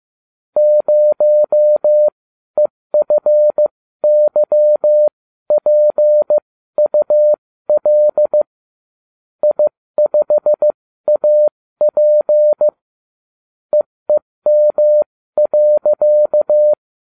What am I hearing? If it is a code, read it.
Morse code, heard as a keyed tone, about 15 wpm, spaced as for 13 wpm, 600 Hz. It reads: 0EFYPUL I5AP EEM.